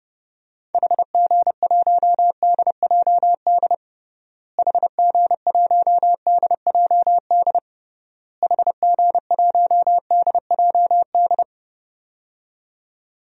Morse code, heard as a keyed tone, about 30 wpm, 700 Hz. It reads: HG1DJD HG1DJD HG1DJD
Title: DXCC-100 Call Signs - 6 Characters - Repeated 3x - 30wpm